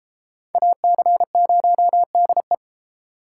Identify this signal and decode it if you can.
Morse code, heard as a keyed tone, about 32 wpm, 700 Hz. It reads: AC0DE